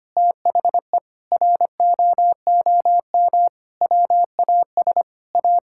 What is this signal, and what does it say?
Morse code, heard as a keyed tone, about 25 wpm, 700 Hz. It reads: THE ROOM WAS A
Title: Sentences from Top 1600 Words 25wpm